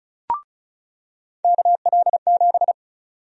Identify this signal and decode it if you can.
Morse code, heard as a keyed tone, about 36 wpm, 700 Hz. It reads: KL7
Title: USA Call Sign Prefixes - 35wpm